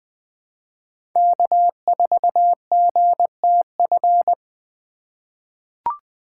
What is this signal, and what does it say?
Morse code, heard as a keyed tone, about 20 wpm, 700 Hz. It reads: K4GTF
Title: USA Call Signs - 5 Characters - 2x Character Spacing - 20wpm